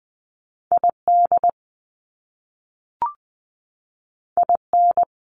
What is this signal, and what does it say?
Morse code, heard as a keyed tone, about 20 wpm, 700 Hz. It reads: ID IN
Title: US States 20wpm